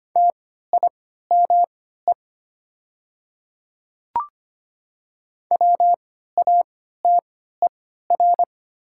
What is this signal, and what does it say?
Morse code, heard as a keyed tone, about 25 wpm, spaced as for 14 wpm, 700 Hz. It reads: TIME WATER
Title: 3x Character Spacing - Top 100 Words - 25wpm